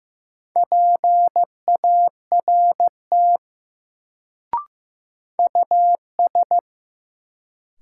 Morse code, heard as a keyed tone, about 15 wpm, 700 Hz. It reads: PART US
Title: Sentences from Top 200 Words - New Word Drill - 15wpm